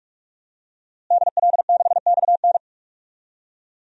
Morse code, heard as a keyed tone, about 46 wpm, 700 Hz. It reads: DL6XN
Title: DXCC-100 Call Signs - 5 Characters - Repeated 3x - 45wpm